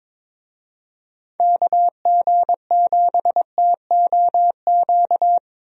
Morse code, heard as a keyed tone, about 22 wpm, 700 Hz. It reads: KG7TOQ
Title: USA Call Signs - 6 Characters - Repeated 3x - 22wpm